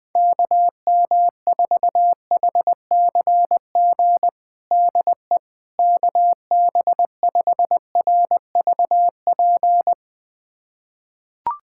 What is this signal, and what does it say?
Morse code, heard as a keyed tone, about 20 wpm, 700 Hz. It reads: KM4HCG DE KB5RVP
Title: USA Callsign DE Callsign 20wpm